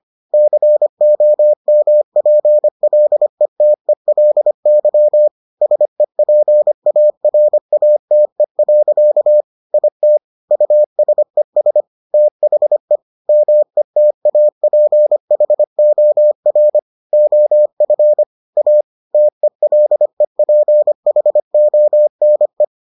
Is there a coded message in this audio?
Yes: Morse code from 0.3 to 22.6 s, heard as a keyed tone, about 25 wpm, 600 Hz. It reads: COMPLETELY SEPARATE. IT USES THE METAPHOR OF A TELEPHONE